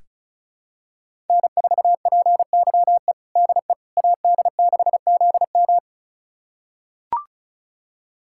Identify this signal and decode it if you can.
Morse code, heard as a keyed tone, about 36 wpm, 700 Hz. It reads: N4PYE DE AD6ZM
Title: USA Callsign DE Callsign - 5 Characters - 35wpm